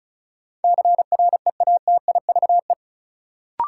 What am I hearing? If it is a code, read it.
Morse code, heard as a keyed tone, about 34 wpm, 700 Hz. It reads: CREATIVE
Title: Sentences from Top 1000 Words - Review Part 5 - New Word Drill - 35wpm